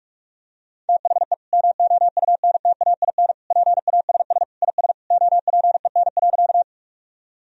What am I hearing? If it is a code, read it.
Morse code, heard as a keyed tone, about 46 wpm, 700 Hz. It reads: THE MOUNTAIN PASS IS OPEN.
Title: Sentences from Top 1000 Words - Review Part 1 - 45wpm